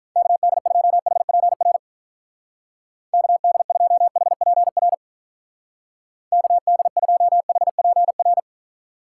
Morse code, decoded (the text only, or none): KD2HPR KD2HPR KD2HPR